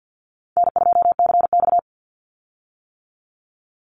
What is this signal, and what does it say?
Morse code, heard as a keyed tone, about 50 wpm, 700 Hz. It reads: N2CX
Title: Easy Call Signs - 50wpm